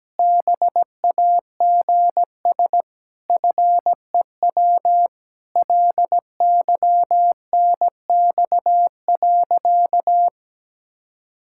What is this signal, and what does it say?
Morse code, heard as a keyed tone, about 17 wpm, 700 Hz. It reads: BAGS FEW LYNX.
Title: Pangram Challenge v2 - 17wpm